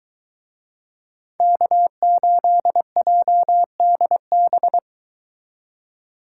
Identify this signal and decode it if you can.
Morse code, heard as a keyed tone, about 23 wpm, 700 Hz. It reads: K8JDB